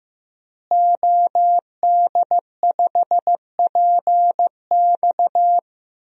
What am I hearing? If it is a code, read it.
Morse code, heard as a keyed tone, about 15 wpm, 700 Hz. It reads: OD5PX
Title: DXCC-100 Call Signs - 5 Characters - Repeated 3x - 15wpm